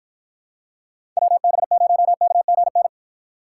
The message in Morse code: WB0KGN

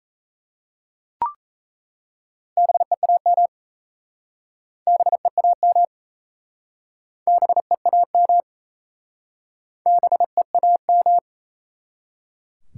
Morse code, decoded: BEAM BEAM BEAM BEAM

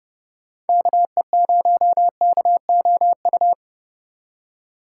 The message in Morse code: KE0KOU

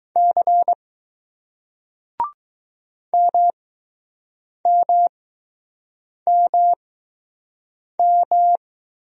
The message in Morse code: C MMMM

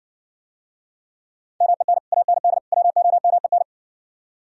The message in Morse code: MEN AND WOMEN

